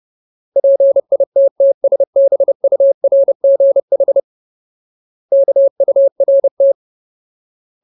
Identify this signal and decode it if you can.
Morse code, heard as a keyed tone, about 30 wpm, 550 Hz. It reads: PITTSBURGH KURT
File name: Ultimate Word List / 30WPM Ultimate